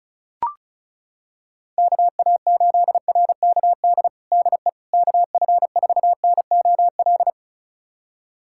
Morse code, heard as a keyed tone, about 36 wpm, 700 Hz. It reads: KA8RKD DE KF4NOL
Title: USA Callsign DE Callsign - 6 Characters - 35wpm